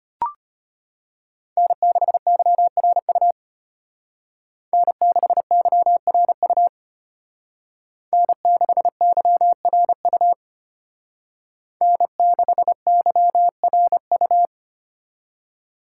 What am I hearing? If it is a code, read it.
Morse code, heard as a keyed tone, about 32 wpm, 700 Hz. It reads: N6YRU N6YRU N6YRU N6YRU